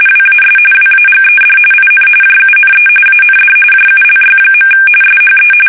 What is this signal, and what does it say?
RTTY, about 45.45 baud. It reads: WAS WAGED IN JUNE AND JULY. SHE
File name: fsk850 / USS-Massachusetts